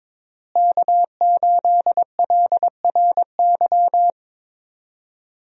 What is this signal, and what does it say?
Morse code, heard as a keyed tone, about 22 wpm, 690 Hz. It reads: K8LRY
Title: USA Call Signs - 5 Characters - Repeated 3x - 22wpm